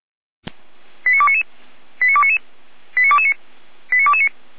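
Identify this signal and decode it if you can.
Five-tone selective call (ZVEI), heard as quick runs of five tones, tones of about 70 ms. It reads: 89299 89299 89229 89229
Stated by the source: ZVEI five-tone selective call received on an SDR.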